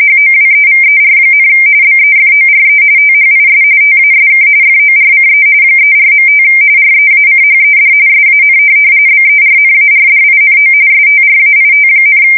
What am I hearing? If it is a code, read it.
RTTY, about 45.45 baud. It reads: HOWEVER, BEFORE THIS TASK FORCE 34.5 REACHED SAN BERNARDINO STRAIT,
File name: fsk170 / USS-New-Jersey